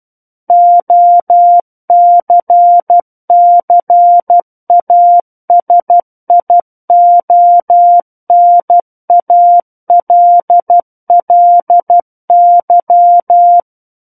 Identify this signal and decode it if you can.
Morse code, heard as a keyed tone, about 12 wpm, 700 Hz. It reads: OCCASIONALLY
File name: Book One / 03-Chapter 3 - 12 WPM